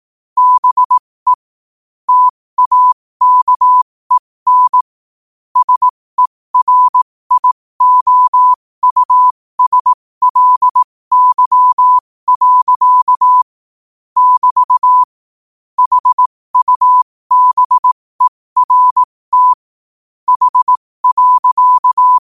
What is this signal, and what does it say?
Morse code, heard as a keyed tone, about 18 wpm, spaced as for 15 wpm, 1000 Hz. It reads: BE TAKEN SERIOUSLY. <BT> HUBERT H.
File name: qotdfiles / 15wpm